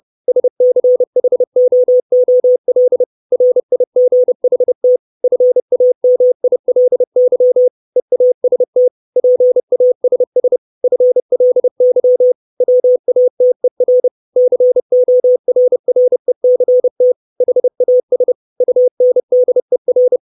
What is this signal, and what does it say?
Morse code, heard as a keyed tone, about 30 wpm, 500 Hz. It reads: SCHOOL RIGHT FAMILY EAST PASS FLY WATER CORRECT HAS UNDER
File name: 632 Words 30WPM